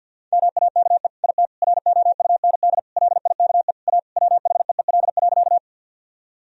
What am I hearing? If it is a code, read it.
Morse code, heard as a keyed tone, about 50 wpm, 700 Hz. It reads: MAKE IT ROUND LIKE A WHEEL.
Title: Sentences from Top 1000 Words - Review Part 3 - 50wpm